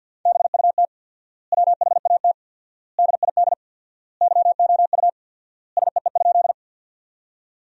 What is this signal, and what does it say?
Morse code, heard as a keyed tone, about 53 wpm, 700 Hz. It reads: BUT WHAT DID YOU SEE?
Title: 4x Word-Spacing - Sentences from Top 100 Words - 50wpm